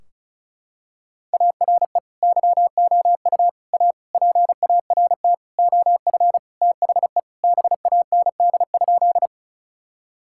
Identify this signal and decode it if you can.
Morse code, heard as a keyed tone, about 36 wpm, 700 Hz. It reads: ARE YOU A PART OF THE BAND?
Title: Sentences from Top 1600 Words 35wpm